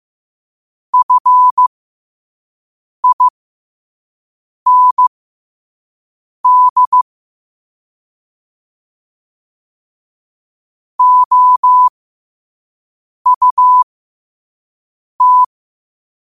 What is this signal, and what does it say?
Morse code, heard as a keyed tone, about 15 wpm, spaced as for 5 wpm, 1000 Hz. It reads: FIND OUT
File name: qotdfiles / 5wpm